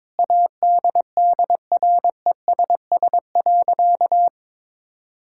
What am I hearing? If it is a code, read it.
Morse code, heard as a keyed tone, about 22 wpm, 700 Hz. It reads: ADDRESS.
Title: Sentences from Top 1000 Words - Review Part 6 - 22wpm